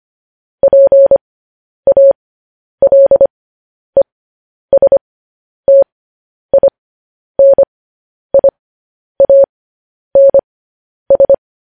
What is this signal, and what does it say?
Morse code, heard as a keyed tone, about 25 wpm, spaced as for 10 wpm, 550 Hz. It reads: PALESTINIANS
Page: news-headlines-in-morse-code-at-10-wpm